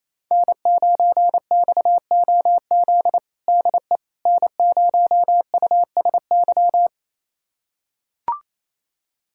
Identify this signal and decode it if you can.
Morse code, heard as a keyed tone, about 28 wpm, 700 Hz. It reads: N9XOZ DE N0USY
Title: USA Callsign DE Callsign - 5 Characters - 28wpm